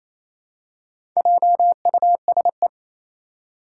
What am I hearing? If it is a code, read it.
Morse code, heard as a keyed tone, about 27 wpm, 700 Hz. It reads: JUSE
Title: Four Challenging Letter-Number Combinations 28wpm